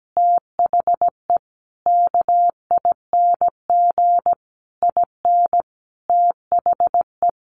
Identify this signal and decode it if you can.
Morse code, heard as a keyed tone, about 17 wpm, 700 Hz. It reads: THE KING IN THE
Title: Sentences from Top 1300 Words 17wpm